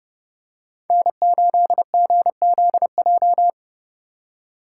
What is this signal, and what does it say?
Morse code, heard as a keyed tone, about 30 wpm, 700 Hz. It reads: N8GZJ
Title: USA Call Signs - 2x Character Spacing - 30wpm